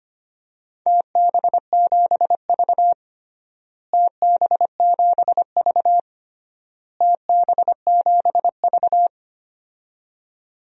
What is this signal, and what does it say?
Morse code, heard as a keyed tone, about 25 wpm, 700 Hz. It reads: TB7V TB7V TB7V
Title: DXCC-100 Call Signs - 4 Characters - Repeated 3x - 25wpm